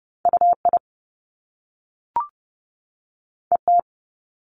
Morse code, heard as a keyed tone, about 30 wpm, 700 Hz. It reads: UI ET
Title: Top Two Letter Combinations in Words 30wpm